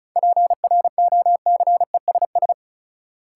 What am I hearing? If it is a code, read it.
Morse code, heard as a keyed tone, about 36 wpm, 700 Hz. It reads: PROCESS
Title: Sentences from Top 1000 Words - Review Part 3 - New Word Drill - 35wpm